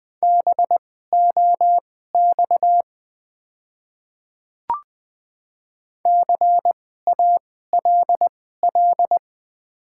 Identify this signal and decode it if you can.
Morse code, heard as a keyed tone, about 20 wpm, spaced as for 14 wpm, 690 Hz. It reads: BOX CALL
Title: 2x Character Spacing - Top 500 Words - 20wpm